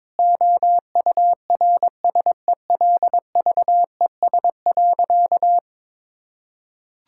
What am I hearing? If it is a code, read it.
Morse code, heard as a keyed tone, about 21 wpm, 700 Hz. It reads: OURSELVES.
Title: Sentences from Top 1600 Words 22wpm